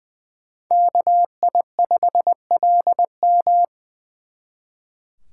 Morse code, heard as a keyed tone, about 24 wpm, 700 Hz. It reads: KI5LM
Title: Speed Racing - USA 2x2 Call Signs 20wpm